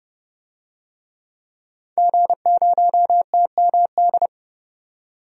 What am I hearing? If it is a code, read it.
Morse code, heard as a keyed tone, about 32 wpm, 700 Hz. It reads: G0TMD
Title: DXCC-100 Call Signs - 5 Characters - 30wpm